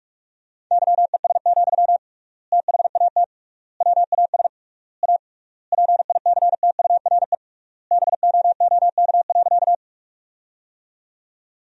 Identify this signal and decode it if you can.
Morse code, heard as a keyed tone, about 42 wpm, 700 Hz. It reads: YES, THAT WAS A PICTURE BOOK.
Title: 3x Word-Spacing - Sentences from Top 500 Words - 45wpm